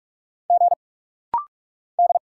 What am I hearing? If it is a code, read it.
Morse code, heard as a keyed tone, about 46 wpm, 700 Hz. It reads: G D
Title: Single Letter-Number - Rapid-Fire 45wpm